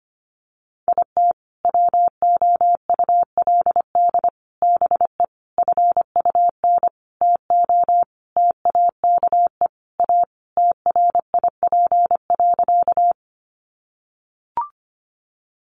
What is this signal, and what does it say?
Morse code, heard as a keyed tone, about 25 wpm, 700 Hz. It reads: IT WOULD BE FUN TO TAKE A TRIP.
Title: Sentences from Top 1300 Words 25wpm